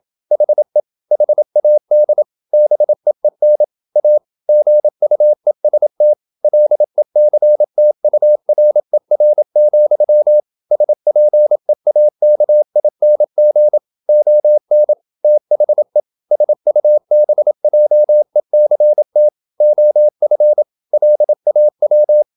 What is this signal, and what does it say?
Morse code, heard as a keyed tone, about 27 wpm, 600 Hz. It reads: HE HAD BEEN A GUEST LECTURER, SPEAKING ON THE SUBJECT OF LAW
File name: Never 0027wpm